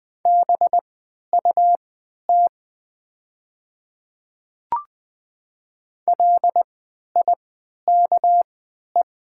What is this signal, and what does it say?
Morse code, heard as a keyed tone, about 20 wpm, spaced as for 11 wpm, 700 Hz. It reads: BUT LIKE
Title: 3x Character Spacing - Top 100 Words - 20wpm